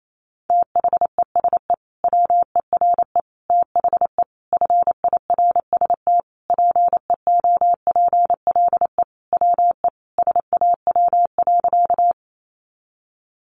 Morse code, heard as a keyed tone, about 28 wpm, 700 Hz. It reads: THESE WERE THE FIRST PEOPLE WE SAW.
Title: Sentences from Top 300 Words 28wpm